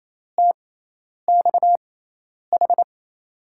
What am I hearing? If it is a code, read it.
Morse code, heard as a keyed tone, about 27 wpm, spaced as for 10 wpm, 700 Hz. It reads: TXH
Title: USA Call Signs - 6 Characters - 6x Character Spacing - 28wpm